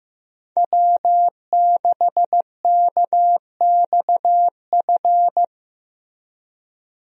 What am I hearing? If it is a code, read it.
Morse code, heard as a keyed tone, about 15 wpm, 700 Hz. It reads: W6KXF
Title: USA Call Signs - 5 Characters - Repeated 2x - 15wpm